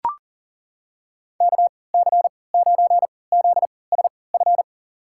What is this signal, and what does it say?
Morse code, heard as a keyed tone, about 40 wpm, spaced as for 23 wpm, 700 Hz. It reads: KC9ZSF
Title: USA Call Signs - 3x Character Spacing - 40wpm